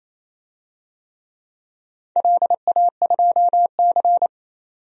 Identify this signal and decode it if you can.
Morse code, heard as a keyed tone, about 28 wpm, 700 Hz. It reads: LA2C
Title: DXCC-100 Call Signs - 4 Characters - 28wpm